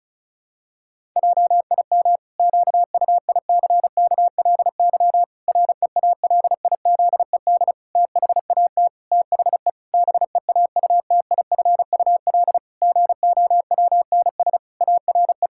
Morse code, heard as a keyed tone, about 36 wpm, 700 Hz. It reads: JIM QUICKLY REALIZED THAT THE BEAUTIFUL GOWNS ARE
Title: Pangram Challenge v2 - 35wpm